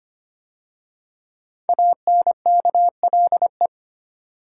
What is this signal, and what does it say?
Morse code, heard as a keyed tone, about 25 wpm, 700 Hz. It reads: ANKLE
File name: 5 Letter Words - First 3 Letters Given - 25wpm